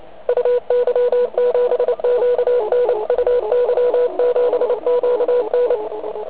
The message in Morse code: UY7QN UY7QN